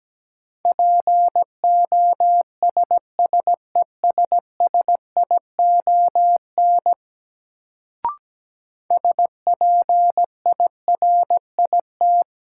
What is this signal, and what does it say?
Morse code, heard as a keyed tone, about 17 wpm, 700 Hz. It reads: POSSESSION SPIRIT
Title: Sentences from Top 1400 Words - New Word Drill Alt - 17wpm